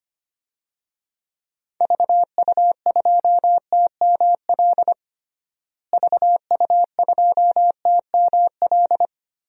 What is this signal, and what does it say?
Morse code, heard as a keyed tone, about 25 wpm, 700 Hz. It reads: VU2TML VU2TML
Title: DXCC-100 Call Signs - 6 Characters - Repeated 2x - 25wpm